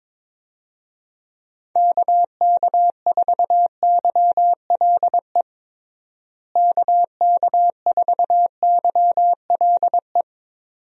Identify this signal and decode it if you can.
Morse code, heard as a keyed tone, about 22 wpm, 700 Hz. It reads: KK4YLE KK4YLE
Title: USA Call Signs - 6 Characters - Repeated 2x - 22wpm